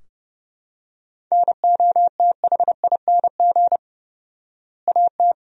Morse code, heard as a keyed tone, about 30 wpm, 700 Hz. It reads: NOTHING AT
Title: 4x Word-Spacing - Sentences from Top 500 Words - 30wpm